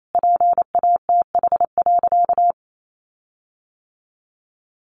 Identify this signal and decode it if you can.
Morse code, heard as a keyed tone, about 28 wpm, 700 Hz. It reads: PATH.
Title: Sentences from Top 1400 Words 28wpm